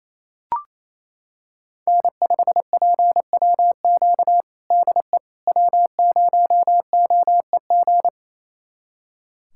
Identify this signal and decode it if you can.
Morse code, heard as a keyed tone, about 28 wpm, 700 Hz. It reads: N5PWQ DE W0OEG
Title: USA Callsign DE Callsign - 5 Characters - 28wpm